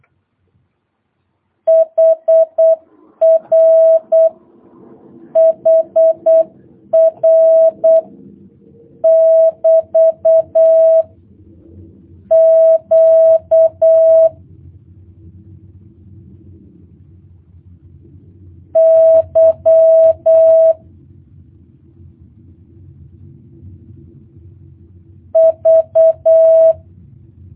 Morse code, heard as a keyed tone, about 8 wpm, 650 Hz. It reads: HR HR <BT> Q Y V
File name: jh8cbh / 2022cw